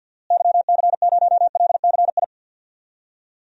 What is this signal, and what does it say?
Morse code, heard as a keyed tone, about 52 wpm, 700 Hz. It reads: YC0LCI
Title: DXCC-100 Call Signs - 6 Characters - Repeated 3x - 50wpm